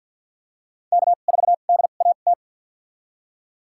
Morse code, heard as a keyed tone, about 53 wpm, spaced as for 37 wpm, 700 Hz. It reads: K4DAT
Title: USA Call Signs - 2x Character Spacing - 50wpm